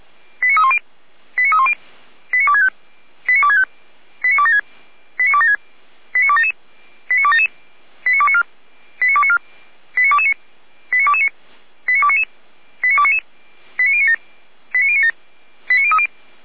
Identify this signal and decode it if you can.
Five-tone selective call (ZVEI), heard as quick runs of five tones, tones of about 70 ms. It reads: 89310 89310 89265 89265 89276 89276 89277 89277 89294 89294 89229 89229 89299 89299 89097 89097 89030
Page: /455/SDR/ZVEI/